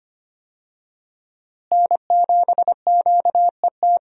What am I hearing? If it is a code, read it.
Morse code, heard as a keyed tone, about 25 wpm, 700 Hz. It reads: N7QET
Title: USA Call Signs - 5 Characters - Repeated 2x - 25wpm